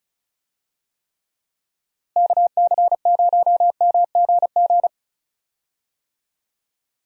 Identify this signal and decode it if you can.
Morse code, heard as a keyed tone, about 36 wpm, 700 Hz. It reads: KC0MGG
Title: USA Call Signs - 6 Characters - Repeated 2x - 35wpm